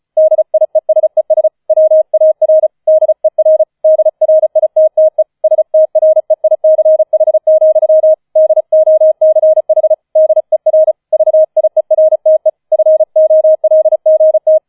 Morse code, heard as a keyed tone, about 36 wpm, 630 Hz. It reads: DIESES WAR DER DRITTE STREICH, DOCH DER VIERTE FOLGT